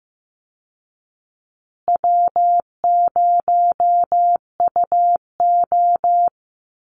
Morse code, heard as a keyed tone, about 15 wpm, 700 Hz. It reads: W0UO
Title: Call Signs Repeated 2x - 15wpm